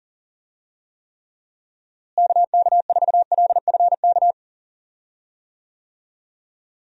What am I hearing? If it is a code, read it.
Morse code, heard as a keyed tone, about 40 wpm, 700 Hz. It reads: KK4LFK
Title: USA Call Signs - 6 Characters - Repeated 3x - 40wpm